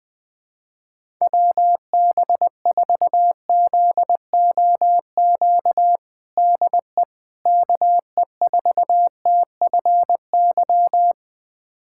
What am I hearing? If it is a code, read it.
Morse code, heard as a keyed tone, about 20 wpm, 700 Hz. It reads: WB4ZOQ DE KE4TFY
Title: USA Callsign DE Callsign 20wpm